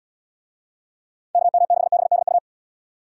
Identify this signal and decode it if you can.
Morse code, heard as a keyed tone, about 79 wpm, 690 Hz. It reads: KM6BDU